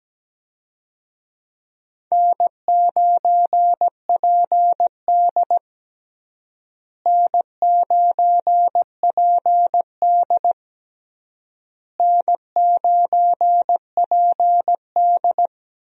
Morse code, heard as a keyed tone, about 17 wpm, 700 Hz. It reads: N9PD N9PD N9PD
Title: USA Call Signs - 4 Characters - Repeated 3x - 17wpm